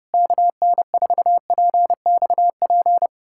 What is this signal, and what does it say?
Morse code, heard as a keyed tone, about 32 wpm, 700 Hz. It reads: KN4PXP
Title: USA Callsign DE Callsign 30wpm